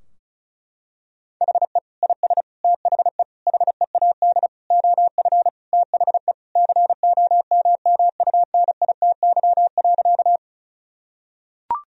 Morse code, heard as a keyed tone, about 34 wpm, 700 Hz. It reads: HE IS THE HEAD OF THE COMMUNITY.
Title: Sentences from Top 1000 Words - Review Part 4 - 35wpm